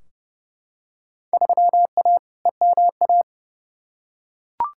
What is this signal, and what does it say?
Morse code, heard as a keyed tone, about 30 wpm, 700 Hz. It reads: 3A EMA
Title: Field Day Exchanges - 30wpm